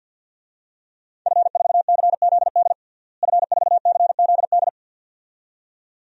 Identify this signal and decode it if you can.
Morse code, heard as a keyed tone, about 45 wpm, 700 Hz. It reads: F4CZD F4CZD
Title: DXCC-100 Call Signs - 5 Characters - Repeated 2x - 50wpm